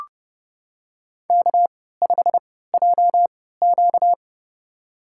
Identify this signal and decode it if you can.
Morse code, heard as a keyed tone, about 30 wpm, spaced as for 17 wpm, 700 Hz. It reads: K5JQ
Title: USA Call Signs - 4 Characters - 3x Character Spacing - 30wpm